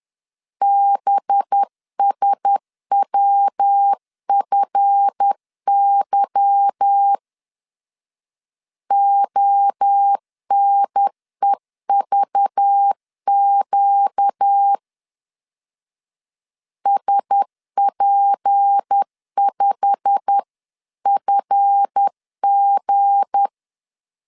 Morse code, heard as a keyed tone, about 10 wpm, 800 Hz. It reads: BSWFY ONEVQ SP5FG
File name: CW / DET / 1 9 1500C S B 10wpm 4439